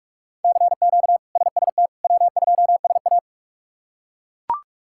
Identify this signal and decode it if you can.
Morse code, heard as a keyed tone, about 42 wpm, 700 Hz. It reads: CQ SST W2SA
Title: K1USN SST QSO Practice - 45wpm